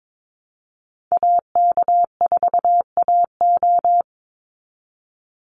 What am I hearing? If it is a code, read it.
Morse code, heard as a keyed tone, about 22 wpm, 700 Hz. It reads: AK4AO
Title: Call Signs Repeated 3x - 22wpm